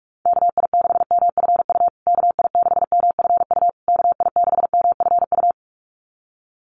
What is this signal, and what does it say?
Morse code, heard as a keyed tone, about 44 wpm, 700 Hz. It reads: KI6MFU KI6MFU KI6MFU
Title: USA Call Signs Repeated 3x 45wpm